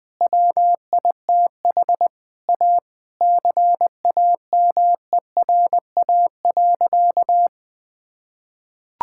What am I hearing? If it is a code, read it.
Morse code, heard as a keyed tone, about 21 wpm, 700 Hz. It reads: WITH A CAMERA.
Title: Sentences from Top 1000 Words - Review Part 5 - 20wpm